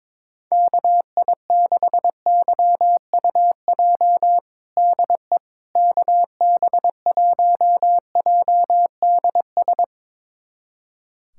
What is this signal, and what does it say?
Morse code, heard as a keyed tone, about 22 wpm, 700 Hz. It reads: KI6YUJ DE KB1JDS
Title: USA Callsign DE Callsign 22wpm